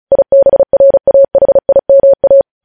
Morse code, heard as a keyed tone, about 36 wpm, 550 Hz. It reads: IBRAHIMA
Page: news headlines in morse code at 35 wpm